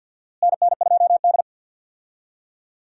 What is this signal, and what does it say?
Morse code, heard as a keyed tone, about 51 wpm, 700 Hz. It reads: NN2D
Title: USA Call Signs - 4 Characters - Repeated 3x - 50wpm